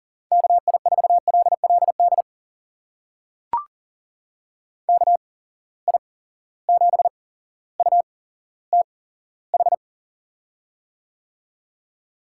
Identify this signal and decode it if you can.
Morse code, heard as a keyed tone, about 40 wpm, spaced as for 17 wpm, 700 Hz. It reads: KI4LLD KI7UTH
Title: USA Call Signs - 6 Characters - 8x Character Spacing - 40wpm